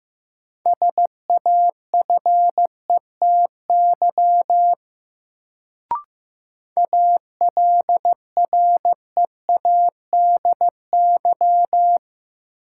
Morse code, heard as a keyed tone, about 15 wpm, 700 Hz. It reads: SAFETY ALREADY
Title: Sentences from Top 700 Words - New Word Drill - 15wpm